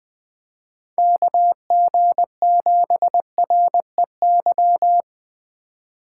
Morse code, heard as a keyed tone, about 24 wpm, 700 Hz. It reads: KG7REY
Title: Speed Racing - USA 2x3 Call Signs 20wpm